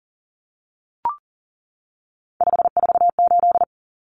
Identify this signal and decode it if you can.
Morse code, heard as a keyed tone, about 40 wpm, 700 Hz. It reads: 548
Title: Numbers - 3-Digit 40wpm